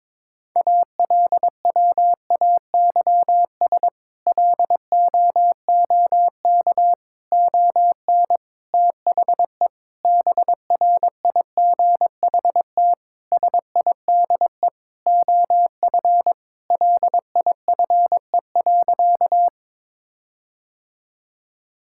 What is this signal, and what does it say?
Morse code, heard as a keyed tone, about 22 wpm, 700 Hz. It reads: ALWAYS LOOK ON THE BRIGHT SIDE OF LIFE.
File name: practice / 22wpm / Sentences with 8-Word Phrases - 22wpm